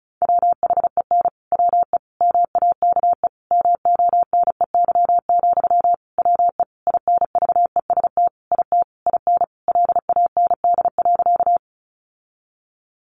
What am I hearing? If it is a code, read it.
Morse code, heard as a keyed tone, about 34 wpm, 700 Hz. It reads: WHEN WE MAKE MONEY, WE INVEST IT IN LAND.
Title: Sentences from Top 1100 Words 35wpm